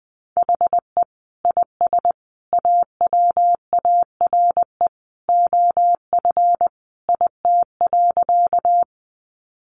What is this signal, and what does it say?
Morse code, heard as a keyed tone, about 20 wpm, 700 Hz. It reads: HE IS AWARE OF IT.
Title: Sentences from Top 1200 Words 20wpm